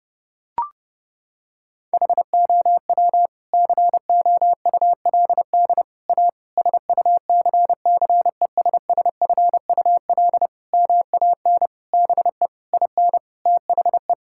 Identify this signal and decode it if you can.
Morse code, heard as a keyed tone, about 30 wpm, 700 Hz. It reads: HOW COULD A SUCCESSFUL MAN BE IN THE
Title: Sentences from Top 1000 Words - Review Part 6 - 30wpm